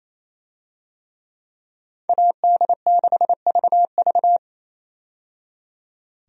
Morse code, heard as a keyed tone, about 28 wpm, 700 Hz. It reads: AD6VV
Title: USA Call Signs - 5 Characters - 28wpm